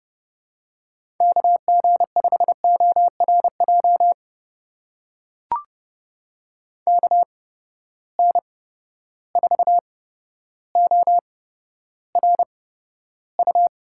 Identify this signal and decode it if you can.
Morse code, heard as a keyed tone, about 31 wpm, spaced as for 13 wpm, 700 Hz. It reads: KG5ORJ KN4ORU
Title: USA Call Signs - 6 Characters - 8x Character Spacing - 30wpm